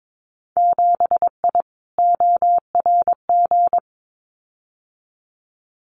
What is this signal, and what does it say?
Morse code, heard as a keyed tone, about 22 wpm, 700 Hz. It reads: 7I ORG
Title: Winter Field Day Exchanges Encore 22wpm